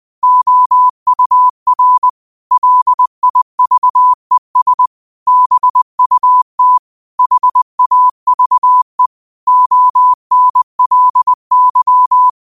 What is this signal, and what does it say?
Morse code, heard as a keyed tone, about 20 wpm, 1000 Hz. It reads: OUR LIVES BUT HAVE ONLY
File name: qotdfiles / 20wpm